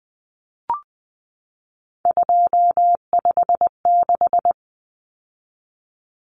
Morse code, heard as a keyed tone, about 20 wpm, 700 Hz. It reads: 256
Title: Common Cut Numbers - 3-Digit 20wpm